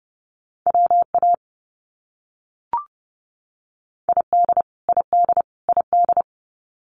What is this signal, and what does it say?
Morse code, heard as a keyed tone, about 30 wpm, 700 Hz. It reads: WA ID ID ID
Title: US States Repeated 3x 30wpm